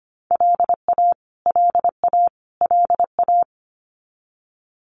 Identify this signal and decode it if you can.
Morse code, heard as a keyed tone, about 25 wpm, 700 Hz. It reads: LA LA LA
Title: US States Repeated 3x 25wpm